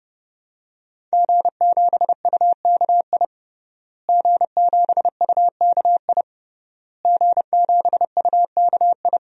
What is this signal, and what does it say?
Morse code, heard as a keyed tone, about 30 wpm, 700 Hz. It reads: G7UKI G7UKI G7UKI
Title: DXCC-100 Call Signs - 5 Characters - Repeated 3x - 30wpm